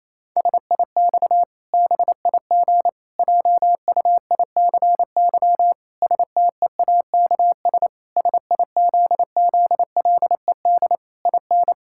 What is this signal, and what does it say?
Morse code, heard as a keyed tone, about 28 wpm, 700 Hz. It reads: SIX BIG JUICY STEAKS SIZZLED IN
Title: Pangram Challenge v2 - 28wpm